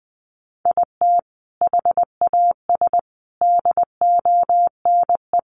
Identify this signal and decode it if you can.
Morse code, heard as a keyed tone, about 20 wpm, 700 Hz. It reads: IT HAS DONE
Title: Sentences from Top 1100 Words 20wpm